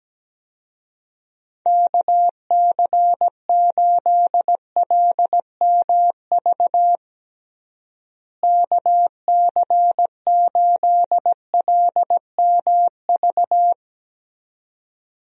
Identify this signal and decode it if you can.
Morse code, heard as a keyed tone, about 17 wpm, 700 Hz. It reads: KC8LMV KC8LMV
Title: USA Call Signs - 6 Characters - Repeated 2x - 17wpm